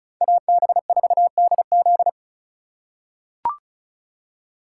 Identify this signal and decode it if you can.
Morse code, heard as a keyed tone, about 36 wpm, 700 Hz. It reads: AB4DZ
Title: USA Call Signs - 3x Character Spacing - 35wpm